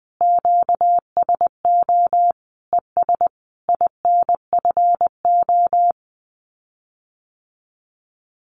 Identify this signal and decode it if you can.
Morse code, heard as a keyed tone, about 20 wpm, 700 Hz. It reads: QSO ES INFO
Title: Common QSO Phrases Encore 20wpm